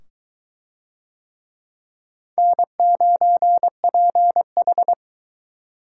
Morse code, heard as a keyed tone, about 22 wpm, 700 Hz. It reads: N9PH